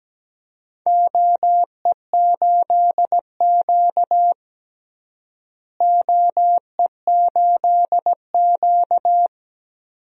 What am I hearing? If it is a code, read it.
Morse code, heard as a keyed tone, about 17 wpm, 700 Hz. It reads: OE8Q OE8Q
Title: DXCC-100 Call Signs - 4 Characters - Repeated 2x - 17wpm